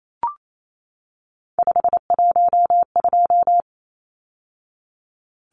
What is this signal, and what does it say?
Morse code, heard as a keyed tone, about 28 wpm, 700 Hz. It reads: 512
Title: Numbers - 3-Digit 28wpm